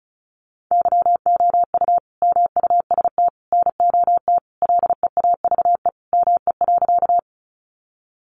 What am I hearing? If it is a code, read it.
Morse code, heard as a keyed tone, about 34 wpm, 700 Hz. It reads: YOU MUST NOT LEAVE ME.
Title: Sentences from Top 600 Words 35wpm